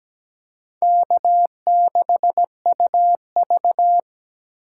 Morse code, heard as a keyed tone, about 20 wpm, 700 Hz. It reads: K6UV
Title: Speed Racing - USA 1x2 Call Signs 17wpm